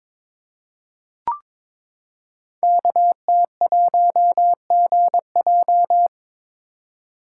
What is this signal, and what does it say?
Morse code, heard as a keyed tone, about 21 wpm, 700 Hz. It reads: KT1GJ
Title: Code Groups - 5 Letter:Numbers - 22wpm